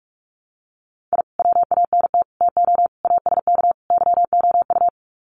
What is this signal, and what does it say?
Morse code, heard as a keyed tone, about 44 wpm, 700 Hz. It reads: I WANT TO ASK YOU
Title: Sets of 5 Words 45wpm